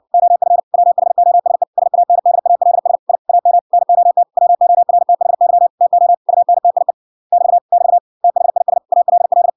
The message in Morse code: OF WHOSE SANDALS I AM NOT WORTHY TO UNTIE <BT> <BT> THIS ALL

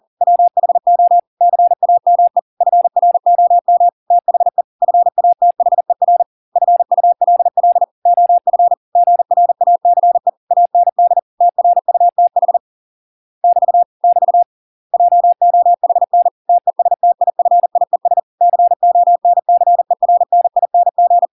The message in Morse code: WHO CAME FROM THE FATHER FULL OF GRACE AND TRUTH <BT> <BT> JOHN TESTIFIES CONCERNING

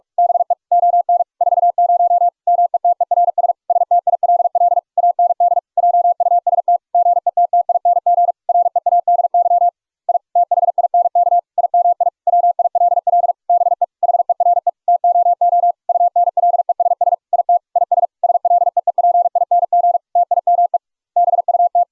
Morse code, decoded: BE ON 40 METERS STILL AND JUST GETTING READY I THINK IMI WILL BE HERE TOO UNLESS IT IS SLEEPING TIME BUT

